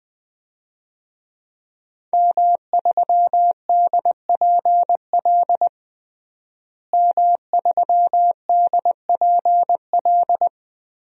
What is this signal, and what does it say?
Morse code, heard as a keyed tone, about 21 wpm, 700 Hz. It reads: M3DPL M3DPL
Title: DXCC-100 Call Signs - 5 Characters - Repeated 3x - 20wpm